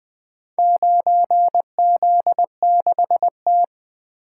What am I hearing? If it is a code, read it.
Morse code, heard as a keyed tone, about 20 wpm, 700 Hz. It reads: 9Z6T
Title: Four Letter-Number Combinations - 20wpm